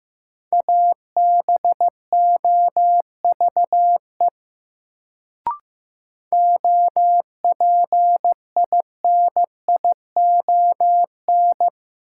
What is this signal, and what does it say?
Morse code, heard as a keyed tone, about 15 wpm, 700 Hz. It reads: ABOVE OPINION
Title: Sentences from Top 1200 Words - New Word Drill Alt - 15wpm